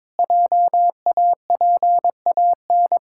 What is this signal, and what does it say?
Morse code, heard as a keyed tone, about 22 wpm, 700 Hz. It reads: JAPAN
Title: Pangram Challenge v2 - 22wpm